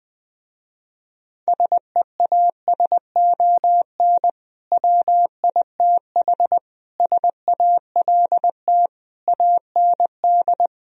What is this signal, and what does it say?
Morse code, heard as a keyed tone, about 20 wpm, 700 Hz. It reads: SEASON WITH SALT AND
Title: Speed-Racing - Sets of 5 Words 17wpm